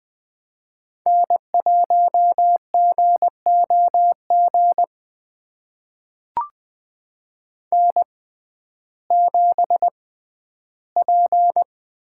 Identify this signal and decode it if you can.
Morse code, heard as a keyed tone, about 20 wpm, spaced as for 12 wpm, 700 Hz. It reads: N1GOG N7P
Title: USA Call Signs - 5 Characters - 6x Character Spacing - 20wpm